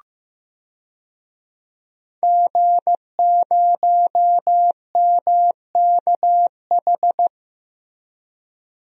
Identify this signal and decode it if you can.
Morse code, heard as a keyed tone, about 15 wpm, 700 Hz. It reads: G0MKH
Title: DXCC-100 Call Signs - 5 Characters - Repeated 2x - 15wpm